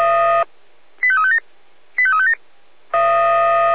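Five-tone selective call (ZVEI), heard as quick runs of five tones, tones of about 70 ms. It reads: 85357 85357
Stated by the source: ZVEI five-tone call 85213 received by SDR.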